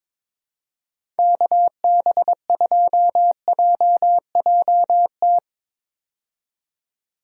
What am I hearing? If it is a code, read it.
Morse code, heard as a keyed tone, about 22 wpm, 700 Hz. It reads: KB2JJT
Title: USA Call Signs - 6 Characters - 22wpm